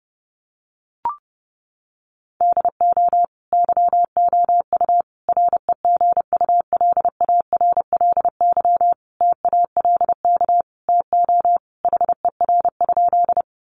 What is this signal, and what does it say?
Morse code, heard as a keyed tone, about 30 wpm, 700 Hz. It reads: DO YOU REGULARLY TALK TO HER?
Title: Sentences from Top 1400 Words 30wpm